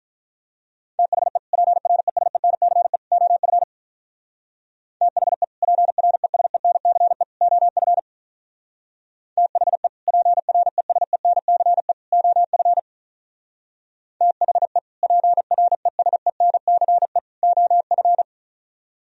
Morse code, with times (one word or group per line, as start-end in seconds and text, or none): 1.0-1.4 THE
1.5-3.0 PRESENCE
3.1-3.6 OF
5.0-5.4 THE
5.6-7.2 PRESENCE
7.4-8.0 OF
9.4-9.9 THE
10.1-11.9 PRESENCE
12.1-12.8 OF
14.2-14.8 THE
15.0-17.2 PRESENCE
17.4-18.2 OF